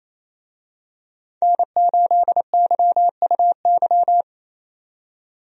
Morse code, heard as a keyed tone, about 28 wpm, 700 Hz. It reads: N8YUY
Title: USA Call Signs - 5 Characters - 28wpm